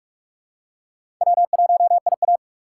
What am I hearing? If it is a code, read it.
Morse code, heard as a keyed tone, about 44 wpm, 700 Hz. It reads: W1IA
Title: USA Call Signs - 4 Characters - 6x Character Spacing - 45wpm